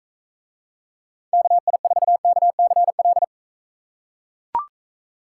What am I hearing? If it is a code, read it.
Morse code, heard as a keyed tone, about 44 wpm, 700 Hz. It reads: KI4KCL